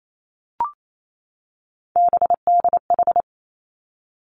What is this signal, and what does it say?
Morse code, heard as a keyed tone, about 28 wpm, 700 Hz. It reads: BDH
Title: Three Challenging Letter-Number Combinations 28wpm